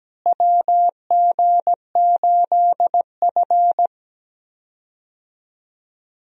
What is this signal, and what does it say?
Morse code, heard as a keyed tone, about 17 wpm, 700 Hz. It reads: WG8F
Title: USA Call Signs - 4 Characters - Repeated 2x - 17wpm